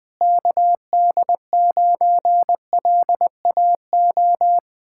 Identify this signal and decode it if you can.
Morse code, heard as a keyed tone, about 20 wpm, 700 Hz. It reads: KD9LAO
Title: Speed-Racing Common QSO Phrases Encore - 17wpm